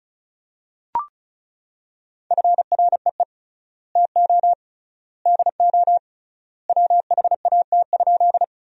Morse code, heard as a keyed tone, about 34 wpm, 700 Hz. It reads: FREE TO DO WHAT?
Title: 3x Word-Spacing - Sentences from Top 500 Words - 35wpm